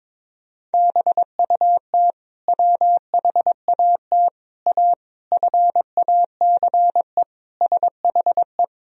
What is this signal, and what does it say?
Morse code, heard as a keyed tone, about 22 wpm, 700 Hz. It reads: BUT WHAT A FACE SHE
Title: Sentences from Top 1000 Words - Review Part 2 - 22wpm